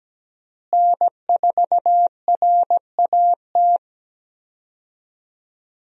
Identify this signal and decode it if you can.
Morse code, heard as a keyed tone, about 17 wpm, 700 Hz. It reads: N4RAT
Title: USA Call Signs - 5 Characters - Repeated 2x - 17wpm